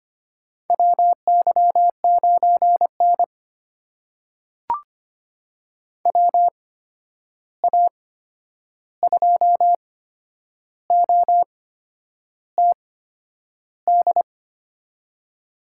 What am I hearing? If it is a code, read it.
Morse code, heard as a keyed tone, about 25 wpm, spaced as for 9 wpm, 700 Hz. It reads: WY9N WA2OTD
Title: USA Call Signs - 8x Character Spacing - 25wpm